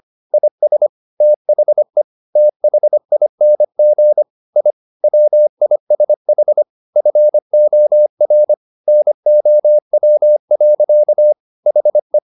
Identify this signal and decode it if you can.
Morse code, heard as a keyed tone, about 25 wpm, 600 Hz. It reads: IS THE THING I WISH FOR NOW. HE